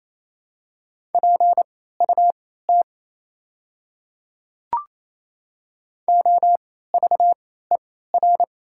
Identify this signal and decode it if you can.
Morse code, heard as a keyed tone, about 27 wpm, spaced as for 16 wpm, 700 Hz. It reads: PUT OVER
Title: 3x Character Spacing - Top 200 Words - 28wpm